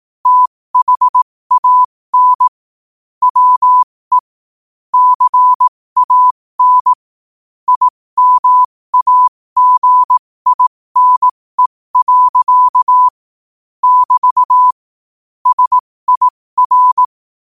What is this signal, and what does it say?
Morse code, heard as a keyed tone, about 18 wpm, spaced as for 15 wpm, 1000 Hz. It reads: THAN WE CAN IMAGINE. <BT> SIR
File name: qotdfiles / 15wpm